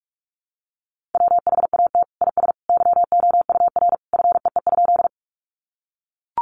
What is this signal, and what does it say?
Morse code, heard as a keyed tone, about 44 wpm, 700 Hz. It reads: WHAT IS YOUR FEE?
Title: Sentences from Top 1500 Words 45wpm